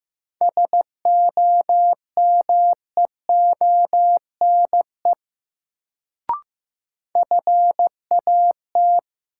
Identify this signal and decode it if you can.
Morse code, heard as a keyed tone, about 15 wpm, 700 Hz. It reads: SOMEONE FAT
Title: Sentences from Top 600 Words - New Word Drill - 15wpm